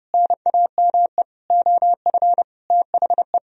Morse code, heard as a keyed tone, about 30 wpm, 700 Hz. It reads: NAME OF THE